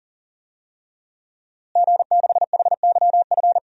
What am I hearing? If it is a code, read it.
Morse code, heard as a keyed tone, about 40 wpm, 700 Hz. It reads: G6HYF
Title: DXCC-100 Call Signs - 5 Characters - Repeated 3x - 40wpm